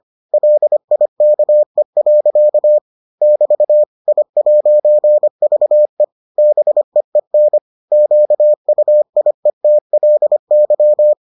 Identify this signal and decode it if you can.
Morse code, heard as a keyed tone, about 25 wpm, 600 Hz. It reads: LIKE. <BT> I'VE BEEN QUIETLY